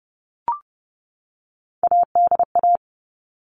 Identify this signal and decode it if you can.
Morse code, heard as a keyed tone, about 30 wpm, 700 Hz. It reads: ADA
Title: All Cut Numbers - 3-Digit 30wpm